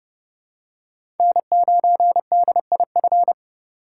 Morse code, heard as a keyed tone, about 30 wpm, 700 Hz. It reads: N9DIF
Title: USA Call Signs - 5 Characters - 3x Character Spacing - 30wpm